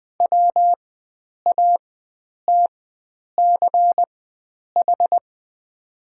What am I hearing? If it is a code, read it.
Morse code, heard as a keyed tone, about 20 wpm, spaced as for 9 wpm, 700 Hz. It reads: WATCH
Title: 4x Character Spacing - Top 500 Words - 20wpm